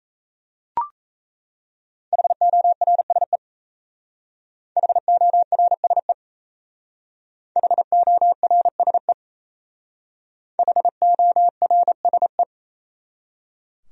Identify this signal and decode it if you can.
Morse code, heard as a keyed tone, about 36 wpm, 700 Hz. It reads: HORSE HORSE HORSE HORSE